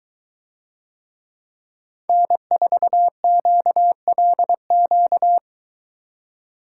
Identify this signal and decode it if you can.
Morse code, heard as a keyed tone, about 22 wpm, 700 Hz. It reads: N4QLQ